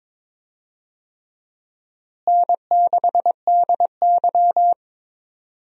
Morse code, heard as a keyed tone, about 22 wpm, 700 Hz. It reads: N6DY
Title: USA Call Signs - 4 Characters - Repeated 3x - 22wpm